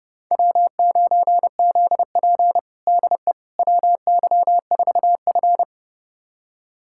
Morse code, heard as a keyed tone, about 32 wpm, 700 Hz. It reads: W9ZP DE WY4F
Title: USA Callsign DE Callsign - 4 Characters - 30wpm